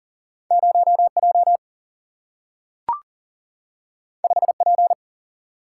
Morse code, heard as a keyed tone, about 41 wpm, 700 Hz. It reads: 0J 5P
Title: Single-Letter Single-Number 40wpm